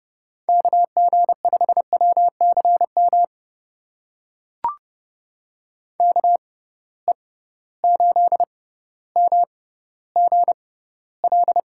Morse code, heard as a keyed tone, about 31 wpm, spaced as for 16 wpm, 700 Hz. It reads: KG5WCM KE8MGL
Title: USA Call Signs - 6 Characters - 6x Character Spacing - 30wpm